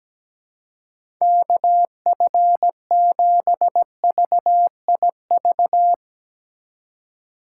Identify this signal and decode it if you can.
Morse code, heard as a keyed tone, about 17 wpm, 700 Hz. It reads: KF7VIV
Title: USA Call Signs - 6 Characters - 4x Character Spacing - 17wpm